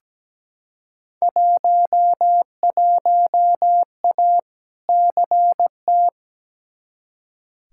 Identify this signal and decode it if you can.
Morse code, heard as a keyed tone, about 17 wpm, 700 Hz. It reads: 11A CT
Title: Field Day Exchanges Encore - 17wpm